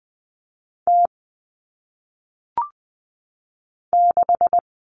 Morse code, heard as a keyed tone, about 20 wpm, 700 Hz. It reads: T 6
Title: All Cut Numbers - 1-Digit 20wpm